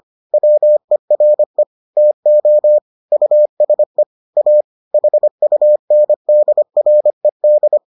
Morse code, heard as a keyed tone, about 25 wpm, 600 Hz. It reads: WERE TO USE A HUNDRED